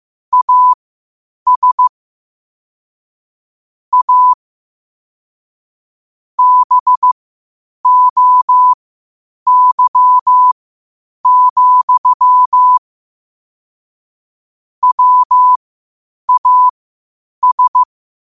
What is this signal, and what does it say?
Morse code, heard as a keyed tone, about 15 wpm, spaced as for 8 wpm, 1000 Hz. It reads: AS A BOY, WAS
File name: qotdfiles / 8wpm